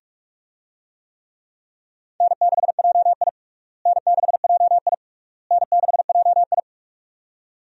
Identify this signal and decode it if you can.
Morse code, heard as a keyed tone, about 46 wpm, 700 Hz. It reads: N6JI N6JI N6JI
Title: USA Call Signs - 4 Characters - Repeated 3x - 45wpm